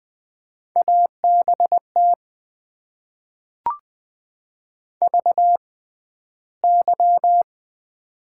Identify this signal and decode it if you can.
Morse code, heard as a keyed tone, about 20 wpm, spaced as for 12 wpm, 700 Hz. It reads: ABT VY
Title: Top 90 QSO Elements - 6x Character Spacing - 20wpm